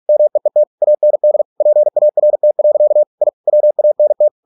Morse code, heard as a keyed tone, about 46 wpm, 600 Hz. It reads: MEET AND PART. I WANT